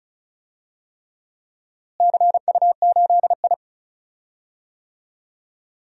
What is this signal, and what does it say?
Morse code, heard as a keyed tone, about 36 wpm, 700 Hz. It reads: CU8I
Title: DXCC-100 Call Signs - 4 Characters - Repeated 3x - 35wpm